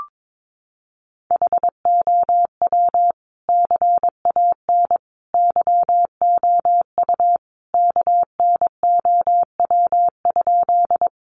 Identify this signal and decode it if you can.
Morse code, heard as a keyed tone, about 22 wpm, 700 Hz. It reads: HOW CAN YOU KNOW?
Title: Sentences from Top 100 Words 22wpm